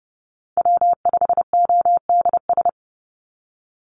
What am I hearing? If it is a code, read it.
Morse code, heard as a keyed tone, about 30 wpm, 700 Hz. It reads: W5ODS
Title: NA SOTA Activator Callsigns 30wpm